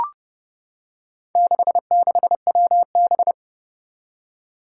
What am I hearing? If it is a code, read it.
Morse code, heard as a keyed tone, about 30 wpm, 700 Hz. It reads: 66WB
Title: Four Challenging Letter-Number Combinations 30wpm